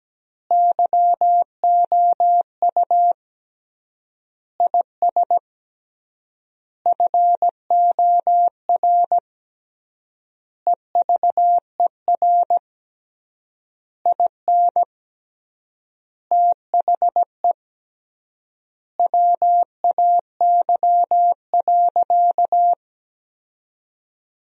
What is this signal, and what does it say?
Morse code, heard as a keyed tone, about 17 wpm, 700 Hz. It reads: YOU IS FOR EVER IN THE WAY.
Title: 3x Word-Spacing - Sentences from Top 400 Words - 17wpm